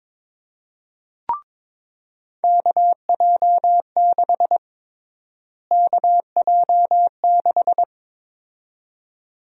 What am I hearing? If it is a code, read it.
Morse code, heard as a keyed tone, about 22 wpm, 700 Hz. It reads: KJ6 KJ6
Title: USA Call Sign Prefixes - Repeated 2x - 22wpm